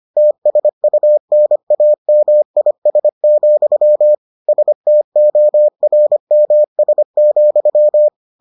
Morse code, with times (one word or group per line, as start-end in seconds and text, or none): 0.2-4.1 TSUNAMIS,
4.5-8.1 STORMS,